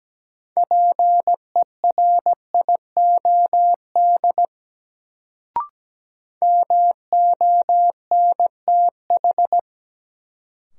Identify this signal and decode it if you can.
Morse code, heard as a keyed tone, about 17 wpm, 700 Hz. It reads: PERIOD MONTH
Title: Sentences from Top 800 Words - New Word Drill - 17wpm